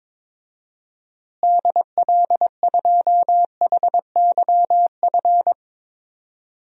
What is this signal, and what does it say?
Morse code, heard as a keyed tone, about 21 wpm, 700 Hz. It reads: DL2HYF
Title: DXCC-100 Call Signs - 6 Characters - Repeated 2x - 22wpm